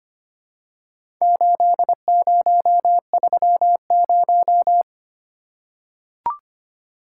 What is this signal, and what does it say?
Morse code, heard as a keyed tone, about 25 wpm, 700 Hz. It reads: 8030
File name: practice / 25wpm / Four Letter-Number Combinations - 25wpm